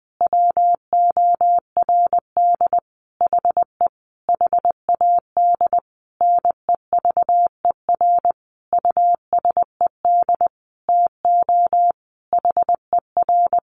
Morse code, heard as a keyed tone, about 20 wpm, 700 Hz. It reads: WORD HE HAD NEVER USED TO HER
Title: Sentences from Top 600 Words 20wpm